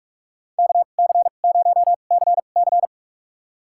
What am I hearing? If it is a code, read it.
Morse code, heard as a keyed tone, about 41 wpm, spaced as for 31 wpm, 700 Hz. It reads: KC0CC
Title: USA Call Signs - 5 Characters - 2x Character Spacing - 45wpm